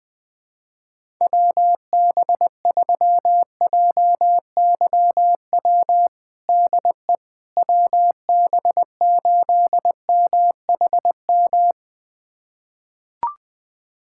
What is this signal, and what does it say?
Morse code, heard as a keyed tone, about 20 wpm, 700 Hz. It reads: WB3JYW DE WB8MHM
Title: USA Callsign DE Callsign - 6 Characters - 20wpm